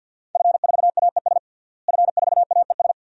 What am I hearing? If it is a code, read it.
Morse code, heard as a keyed tone, about 47 wpm, 700 Hz. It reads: F4AES F4AES
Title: DXCC-100 Call Signs - 5 Characters - Repeated 2x - 50wpm